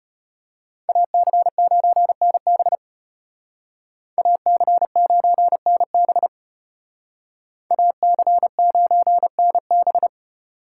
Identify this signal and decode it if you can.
Morse code, heard as a keyed tone, about 33 wpm, 700 Hz. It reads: AC9NB AC9NB AC9NB